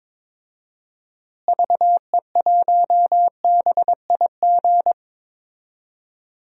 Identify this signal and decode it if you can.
Morse code, heard as a keyed tone, about 22 wpm, 700 Hz. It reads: VE1BIG
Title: DXCC-100 Call Signs - 6 Characters - Repeated 3x - 22wpm